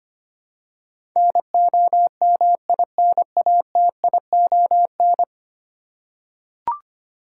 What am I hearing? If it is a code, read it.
Morse code, heard as a keyed tone, about 25 wpm, 700 Hz. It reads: NOMINATION
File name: practice / 25wpm / Ten Letter Words - 25wpm